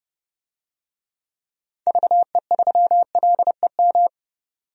Morse code, heard as a keyed tone, about 30 wpm, 700 Hz. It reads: VE3LEM
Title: DXCC-100 Call Signs - 6 Characters - 30wpm